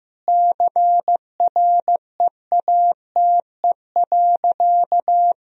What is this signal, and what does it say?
Morse code, heard as a keyed tone, about 15 wpm, 700 Hz. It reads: CREATE.
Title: Sentences from Top 1000 Words - Review Part 3 - 15wpm